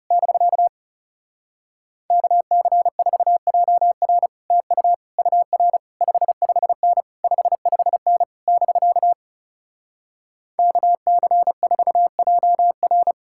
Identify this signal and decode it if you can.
Morse code, heard as a keyed tone, about 37 wpm, 700 Hz. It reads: <BK> KC4JR TU UR 55N 55N <BK> KC4JR
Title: Speed-Racing Common QSO Phrases - 30wpm